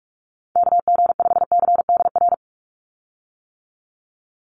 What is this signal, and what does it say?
Morse code, heard as a keyed tone, about 44 wpm, 700 Hz. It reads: KG5CDR
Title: USA Call Signs 45wpm